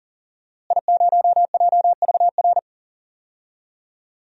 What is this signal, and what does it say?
Morse code, heard as a keyed tone, about 40 wpm, 700 Hz. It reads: I0JVR
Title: DXCC-100 Call Signs - 5 Characters - Repeated 3x - 40wpm